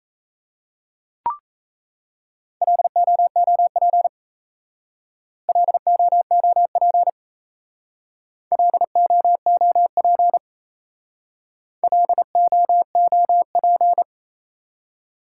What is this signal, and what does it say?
Morse code, heard as a keyed tone, about 34 wpm, 700 Hz. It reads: LOOP LOOP LOOP LOOP
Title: Speed-Racing Top 180 QSO Elements 28wpm